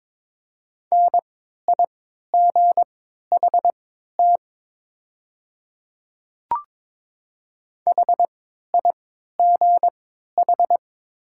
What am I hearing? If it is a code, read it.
Morse code, heard as a keyed tone, about 22 wpm, spaced as for 13 wpm, 700 Hz. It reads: NIGHT HIGH
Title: 3x Character Spacing - Top 300 Words - 22wpm